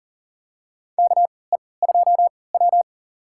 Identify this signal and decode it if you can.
Morse code, heard as a keyed tone, about 40 wpm, spaced as for 23 wpm, 700 Hz. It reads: KE2W
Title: USA Call Signs - 3x Character Spacing - 40wpm